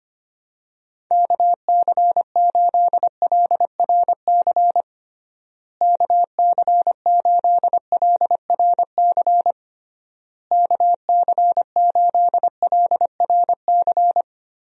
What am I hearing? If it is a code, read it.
Morse code, heard as a keyed tone, about 25 wpm, 700 Hz. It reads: KC8LRC KC8LRC KC8LRC
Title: USA Call Signs - 6 Characters - Repeated 3x - 25wpm